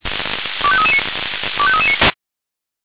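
Five-tone selective call (ZVEI), heard as quick runs of five tones, tones of about 70 ms. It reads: 25339 25339
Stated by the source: Five-tone ZVEI selective call received on an SDR.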